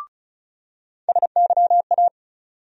Morse code, heard as a keyed tone, about 34 wpm, 700 Hz. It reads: SYA
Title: USA Call Sign Postfixes - 3 Letters 35wpm